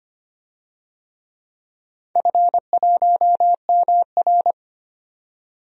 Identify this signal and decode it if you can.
Morse code, heard as a keyed tone, about 25 wpm, 700 Hz. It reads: F1MR
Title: DXCC-100 Call Signs - 4 Characters - 25wpm